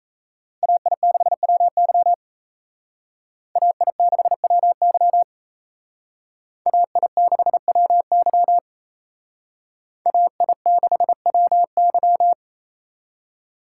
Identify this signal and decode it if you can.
Morse code, heard as a keyed tone, about 32 wpm, 700 Hz. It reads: AI6WY AI6WY AI6WY AI6WY
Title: Speed Racing - USA 2x2 Call Signs 28wpm